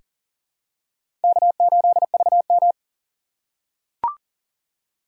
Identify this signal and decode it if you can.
Morse code, heard as a keyed tone, about 40 wpm, 700 Hz. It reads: K8VM
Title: USA Call Signs - 4 Characters - 2x Character Spacing - 40wpm